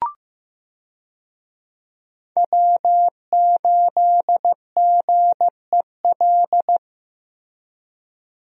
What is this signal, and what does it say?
Morse code, heard as a keyed tone, about 15 wpm, 700 Hz. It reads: W8GEL
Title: USA Call Signs - 5 Characters - Repeated 2x - 15wpm